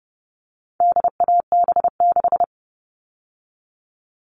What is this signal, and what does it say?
Morse code, heard as a keyed tone, about 30 wpm, 700 Hz. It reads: DAB6
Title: All Cut Numbers - 4-Digit 30wpm